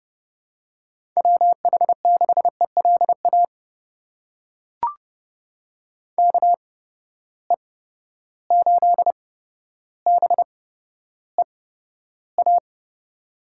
Morse code, heard as a keyed tone, about 30 wpm, spaced as for 13 wpm, 700 Hz. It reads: WH6ELA KE8BEA
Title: USA Call Signs - 6 Characters - 8x Character Spacing - 30wpm